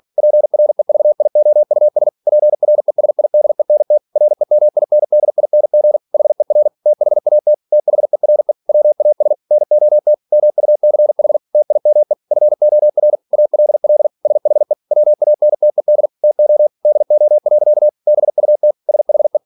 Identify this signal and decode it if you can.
Morse code, heard as a keyed tone, about 46 wpm, 600 Hz. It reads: PREVIOUS PRESIDENT REMINDING HER THAT THERE WAS NOT MUCH TIME FOR ALL SHE WANTED TO DO. BUT SHE
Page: Never 47wpm